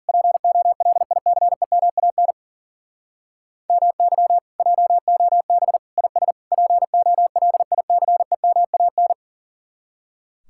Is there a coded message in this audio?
Yes: Morse code from 0.1 to 9.1 s, heard as a keyed tone, about 51 wpm, 700 Hz. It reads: POLICEMAN MY JOB IS POLICEMAN